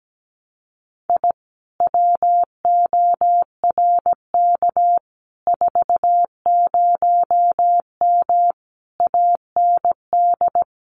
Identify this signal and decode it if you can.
Morse code, heard as a keyed tone, about 17 wpm, 700 Hz. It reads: I WORK 40M AND
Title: Common QSO Phrases Encore 17wpm